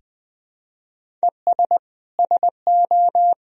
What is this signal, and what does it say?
Morse code, heard as a keyed tone, about 21 wpm, 700 Hz. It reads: ES SO
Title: Sentences from Top 100 Words and Abbreviations 20wpm